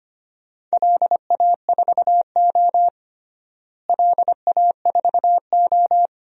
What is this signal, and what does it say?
Morse code, heard as a keyed tone, about 25 wpm, 700 Hz. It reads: LA4O LA4O
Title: DXCC-100 Call Signs - 4 Characters - Repeated 3x - 25wpm